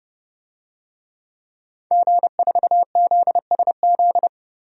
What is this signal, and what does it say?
Morse code, heard as a keyed tone, about 29 wpm, 700 Hz. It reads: G4ZSZ
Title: DXCC-100 Call Signs - 5 Characters - Repeated 3x - 30wpm